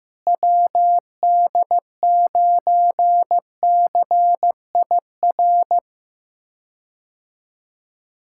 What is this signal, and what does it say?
Morse code, heard as a keyed tone, about 15 wpm, 700 Hz. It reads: WD9CIR
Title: K1USN SST QSO Practice - 15wpm